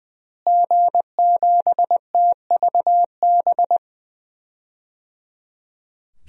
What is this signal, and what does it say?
Morse code, heard as a keyed tone, about 21 wpm, 700 Hz. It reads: G7TVB
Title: DXCC-100 Call Signs - 5 Characters - Repeated 3x - 20wpm